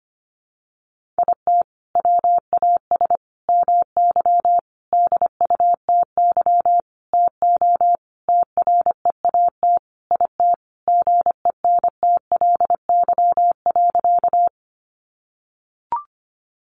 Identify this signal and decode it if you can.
Morse code, heard as a keyed tone, about 25 wpm, 700 Hz. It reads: IT WAS MY DUTY TO TREAT IT GENTLY.
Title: Sentences from Top 1500 Words 25wpm